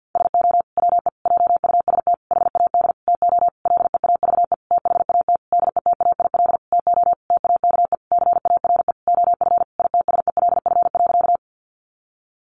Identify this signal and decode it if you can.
Morse code, heard as a keyed tone, about 50 wpm, 700 Hz. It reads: SO WE JUST HAD TO LEAVE THAT DETAIL TO TAKE CARE OF ITSELF.
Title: Sentences from Top 1500 Words 50wpm